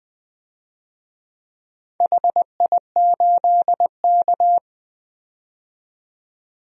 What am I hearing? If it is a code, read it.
Morse code, heard as a keyed tone, about 21 wpm, 700 Hz. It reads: HI8K
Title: DXCC-100 Call Signs - 4 Characters - 20wpm